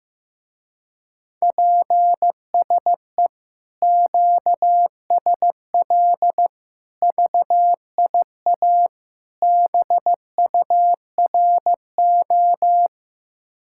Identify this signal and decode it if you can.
Morse code, heard as a keyed tone, about 15 wpm, 700 Hz. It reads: PSE QSL VIA BURO
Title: Speed-Racing Common QSO Phrases Encore - 15wpm